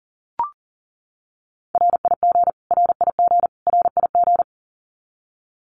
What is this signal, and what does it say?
Morse code, heard as a keyed tone, about 40 wpm, 700 Hz. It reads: RIG RIG RIG
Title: Abbreviations Repeated 3x - 40wpm